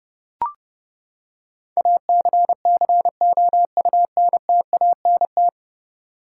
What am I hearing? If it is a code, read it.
Morse code, heard as a keyed tone, about 30 wpm, 700 Hz. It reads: ACCOUNTANT
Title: K1USN SST QSO Practice 30wpm